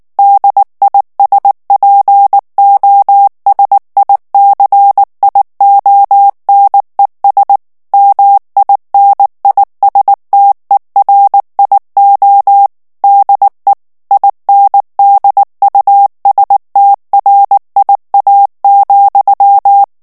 Morse code, heard as a keyed tone, about 19 wpm, 800 Hz. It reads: DISPOSICIONES MINISTERIO DE INDUSTRIA,
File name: ESPAGNOL TEXTE EN CLAIR 2 VITESSE 18 MOTS MINUTES